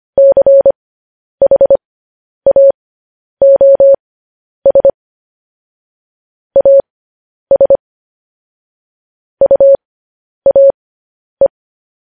Morse code, heard as a keyed tone, about 25 wpm, spaced as for 10 wpm, 550 Hz. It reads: CHAOS AS UAE
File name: podcast / News headlines-1x-25x10